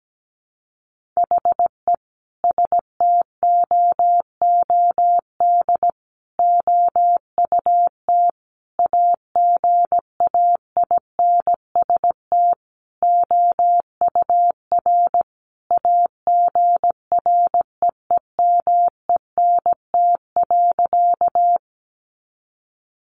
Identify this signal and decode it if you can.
Morse code, heard as a keyed tone, about 17 wpm, 700 Hz. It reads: HE STOOD OUT AGAINST OUR AGREEMENT.
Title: Sentences from Top 1500 Words 17wpm